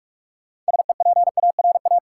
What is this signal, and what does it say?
Morse code, heard as a keyed tone, about 46 wpm, 700 Hz. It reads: SEPARA
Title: Sets of 5 Words - New Word Drill - 45wpm